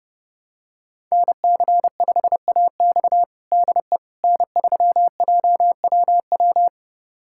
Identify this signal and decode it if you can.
Morse code, heard as a keyed tone, about 30 wpm, 700 Hz. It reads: NC5AX DE N3JWW
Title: USA Callsign DE Callsign - 5 Characters - 30wpm